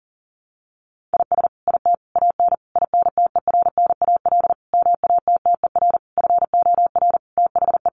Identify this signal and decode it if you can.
Morse code, heard as a keyed tone, about 40 wpm, 700 Hz. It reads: IS IT AN INTERNAL MATTER FOR THE
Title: Sentences from Top 1400 Words 40wpm